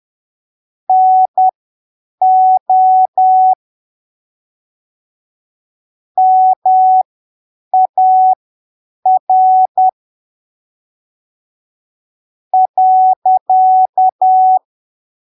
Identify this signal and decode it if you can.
Morse code, heard as a keyed tone, about 10 wpm, spaced as for 7 wpm, 750 Hz. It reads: NO MAR .
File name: arquivos / texto / texto 03 10ppm